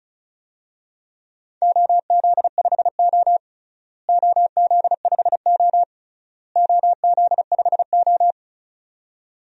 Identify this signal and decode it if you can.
Morse code, heard as a keyed tone, about 36 wpm, 700 Hz. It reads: OZ5O OZ5O OZ5O
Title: DXCC-100 Call Signs - 4 Characters - Repeated 3x - 35wpm